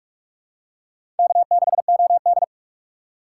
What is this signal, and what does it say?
Morse code, heard as a keyed tone, about 45 wpm, 700 Hz. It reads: K6OD
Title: USA Call Signs - 6x Character Spacing - 45wpm